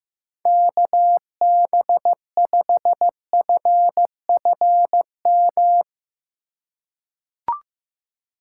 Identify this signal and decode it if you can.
Morse code, heard as a keyed tone, about 15 wpm, 700 Hz. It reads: KB5FFM
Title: USA Call Signs - 4x Character Spacing - 15wpm